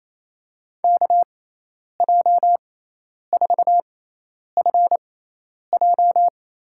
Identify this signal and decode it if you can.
Morse code, heard as a keyed tone, about 28 wpm, spaced as for 10 wpm, 700 Hz. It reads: KJ4FJ
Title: USA Call Signs - 6 Characters - 6x Character Spacing - 28wpm